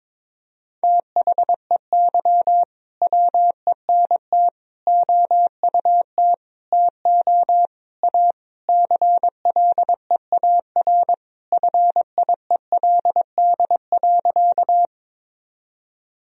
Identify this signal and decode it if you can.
Morse code, heard as a keyed tone, about 22 wpm, 700 Hz. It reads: THEY WENT OUT TO A CLEAR FIELD.
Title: Sentences from Top 1000 Words - Review Part 2 - 22wpm